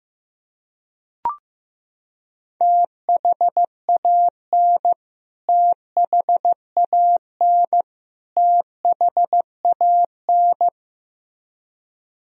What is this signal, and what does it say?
Morse code, heard as a keyed tone, about 15 wpm, 700 Hz. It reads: THAN THAN THAN
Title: Top 100 Words Repeated 15wpm